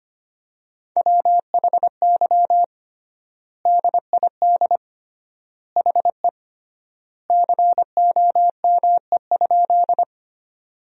Morse code, heard as a keyed tone, about 25 wpm, 700 Hz. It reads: WHY DID HE COME?
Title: 3x Word-Spacing - Sentences from Top 200 Words - 25wpm